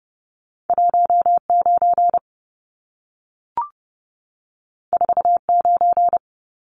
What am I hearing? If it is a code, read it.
Morse code, heard as a keyed tone, about 30 wpm, 700 Hz. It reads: 19 49
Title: Numbers - 2-Digit 30wpm